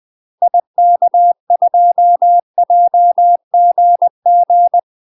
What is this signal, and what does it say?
Morse code, heard as a keyed tone, about 20 wpm, 700 Hz. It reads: IK2JGG
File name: UK MASTER QSOs TRAINING / 20WPM / UK MASTER QSOs 0002